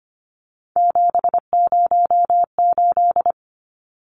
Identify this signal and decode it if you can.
Morse code, heard as a keyed tone, about 25 wpm, 700 Hz. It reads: 708
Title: Numbers - 3-Digit 25wpm